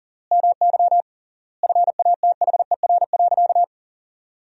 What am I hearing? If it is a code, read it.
Morse code, heard as a keyed tone, about 37 wpm, 700 Hz. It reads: MY FATHER.
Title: 3x Word-Spacing - Sentences from Top 500 Words - 40wpm